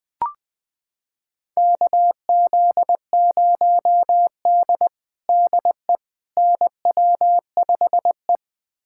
Morse code, heard as a keyed tone, about 21 wpm, 700 Hz. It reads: KZ0D DE NW5E
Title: USA Callsign DE Callsign - 4 Characters - 20wpm